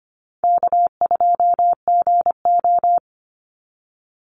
Morse code, heard as a keyed tone, about 25 wpm, 700 Hz. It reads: K2GO
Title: Easy Call Signs Repeated 3x - 25wpm